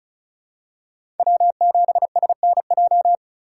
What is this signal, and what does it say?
Morse code, heard as a keyed tone, about 36 wpm, 700 Hz. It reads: W7SNJ
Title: USA Call Signs - 5 Characters - 2x Character Spacing - 35wpm